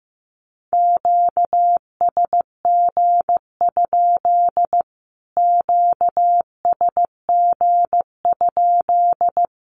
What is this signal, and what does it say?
Morse code, heard as a keyed tone, about 15 wpm, 700 Hz. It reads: QSG? QSG?
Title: Q-Codes Repeated 3x 15wpm